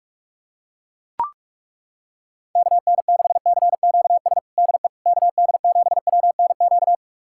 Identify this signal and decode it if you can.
Morse code, heard as a keyed tone, about 41 wpm, 700 Hz. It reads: KN6CQS DE KD7WNQ
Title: USA Callsign DE Callsign - 6 Characters - 45wpm